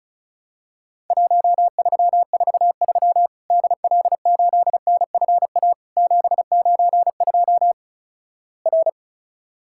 Morse code, heard as a keyed tone, about 36 wpm, 700 Hz. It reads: 1343 DL8NFA 792 R
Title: Worked All Europe DX Contest - QTC Exchange - 35wpm